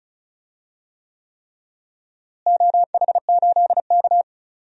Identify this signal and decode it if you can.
Morse code, heard as a keyed tone, about 36 wpm, 700 Hz. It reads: OH8K
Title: DXCC-100 Call Signs - 4 Characters - Repeated 3x - 35wpm